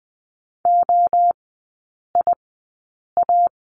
Morse code, heard as a keyed tone, about 20 wpm, spaced as for 8 wpm, 700 Hz. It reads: OIA
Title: 035 Lesson 7e - Letter s - Three Characters 20wpm